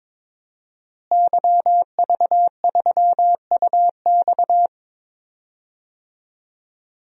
Morse code, heard as a keyed tone, about 21 wpm, 700 Hz. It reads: YV3UX
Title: DXCC-100 Call Signs - 5 Characters - Repeated 3x - 22wpm